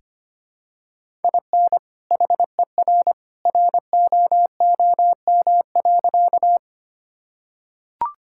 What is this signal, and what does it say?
Morse code, heard as a keyed tone, about 25 wpm, 700 Hz. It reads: IN HER ROOM.
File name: practice / 25wpm / Sentences with 2-Word Phrases - 25wpm